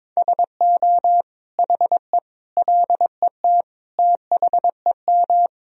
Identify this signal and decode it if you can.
Morse code, heard as a keyed tone, about 22 wpm, 700 Hz. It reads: SO HE LET THEM
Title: Sentences from Top 1000 Words - Review Part 2 - 22wpm